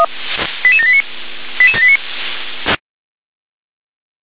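Five-tone selective call (ZVEI), heard as quick runs of five tones, tones of about 70 ms. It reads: 88789 88789
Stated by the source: ZVEI five-tone selective call received by SDR.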